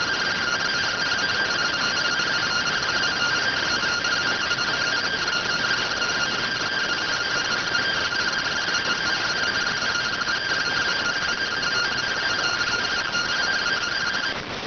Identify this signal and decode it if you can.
Two-tone teleprinter signal, about 170 Hz shift.